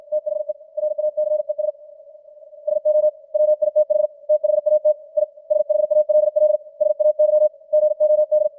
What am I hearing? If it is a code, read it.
Morse code, heard as a keyed tone, about 56 wpm, 600 Hz. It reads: THE RACES SO WITH THAT I SHALL SAY GOD
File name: archived / CWBros